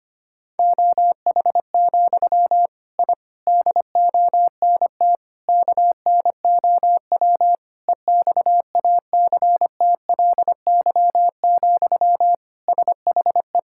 Morse code, heard as a keyed tone, about 25 wpm, 700 Hz. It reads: OH, I DONT KNOW EXACTLY, SHE
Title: Sentences from Top 1000 Words - Review Part 6 - 25wpm